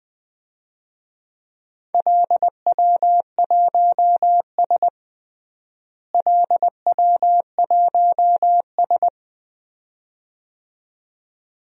Morse code, heard as a keyed tone, about 20 wpm, 700 Hz. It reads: LW1S LW1S
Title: DXCC-100 Call Signs - 4 Characters - Repeated 2x - 20wpm